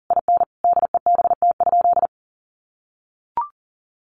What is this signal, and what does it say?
Morse code, heard as a keyed tone, about 40 wpm, 700 Hz. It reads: IN DEBT?
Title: Sentences from Top 1300 Words 40wpm